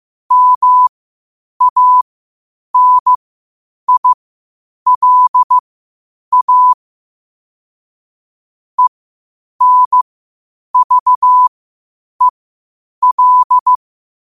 Morse code, heard as a keyed tone, about 15 wpm, spaced as for 8 wpm, 1000 Hz. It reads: MANILA ENVEL
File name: qotdfiles / 8wpm